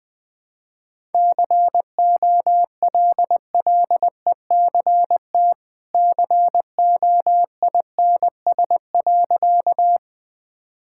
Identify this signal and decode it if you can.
Morse code, heard as a keyed tone, about 21 wpm, 700 Hz. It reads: COLLECT COINS.
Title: Easy Sentences with Hints - 20wpm